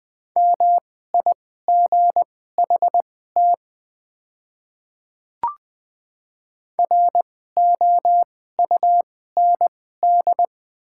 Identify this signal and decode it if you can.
Morse code, heard as a keyed tone, about 20 wpm, spaced as for 14 wpm, 700 Hz. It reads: MIGHT ROUND
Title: 2x Character Spacing - Top 300 Words - 20wpm